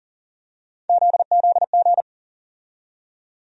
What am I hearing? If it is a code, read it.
Morse code, heard as a keyed tone, about 40 wpm, 700 Hz. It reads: ZZG
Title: USA Call Sign Postfixes - 3 Letters 40wpm